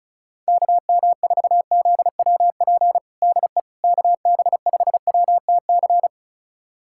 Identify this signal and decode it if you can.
Morse code, heard as a keyed tone, about 34 wpm, 700 Hz. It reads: KM4ZWP DE KB5WTC
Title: USA Callsign DE Callsign 35wpm